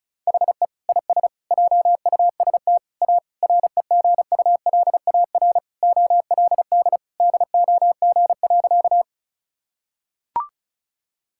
Morse code, heard as a keyed tone, about 36 wpm, 700 Hz. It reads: HE IS JUST A REGULAR OLD DOG.
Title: Sentences from Top 1000 Words - Review Part 6 - 35wpm